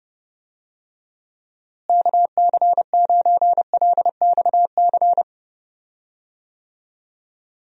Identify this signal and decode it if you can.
Morse code, heard as a keyed tone, about 31 wpm, 700 Hz. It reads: KC9LXC
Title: USA Call Signs - 6 Characters - Repeated 2x - 30wpm